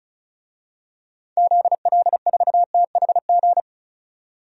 Morse code, heard as a keyed tone, about 34 wpm, 700 Hz. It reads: ZL4THG
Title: DXCC-100 Call Signs - 6 Characters - 35wpm